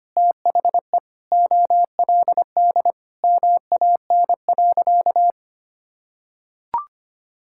Morse code, heard as a keyed tone, about 25 wpm, 700 Hz. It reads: THE OLD MAN.
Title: Sentences from Top 1600 Words 25wpm